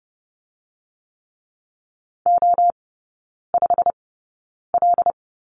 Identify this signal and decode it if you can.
Morse code, heard as a keyed tone, about 30 wpm, spaced as for 9 wpm, 690 Hz. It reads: O5L
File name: Lesson 17e - Letter u - Three Characters 30wpm